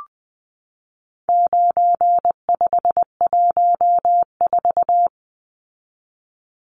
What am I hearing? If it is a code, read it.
Morse code, heard as a keyed tone, about 20 wpm, 700 Hz. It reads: 9514
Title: Numbers - 4-Digit 20wpm